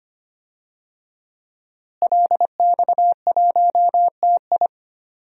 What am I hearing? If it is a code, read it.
Morse code, heard as a keyed tone, about 25 wpm, 700 Hz. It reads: LX1TI
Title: DXCC-100 Call Signs - 5 Characters - Repeated 3x - 25wpm